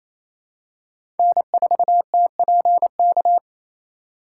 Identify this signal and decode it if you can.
Morse code, heard as a keyed tone, about 28 wpm, 700 Hz. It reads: N4TPK
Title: USA Call Signs - 2x Character Spacing - 28wpm